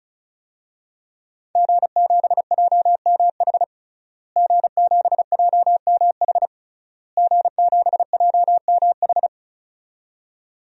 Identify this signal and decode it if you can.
Morse code, heard as a keyed tone, about 36 wpm, 700 Hz. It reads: G7JMH G7JMH G7JMH
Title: DXCC-100 Call Signs - 5 Characters - Repeated 3x - 35wpm